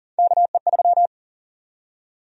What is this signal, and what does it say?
Morse code, heard as a keyed tone, about 40 wpm, 700 Hz. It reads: KE3
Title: USA Call Sign Prefixes - 40wpm